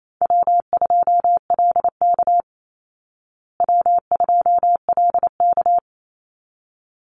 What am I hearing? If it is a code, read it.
Morse code, heard as a keyed tone, about 28 wpm, 700 Hz. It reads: W2LK W2LK
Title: Call Signs Repeated 3x - 28wpm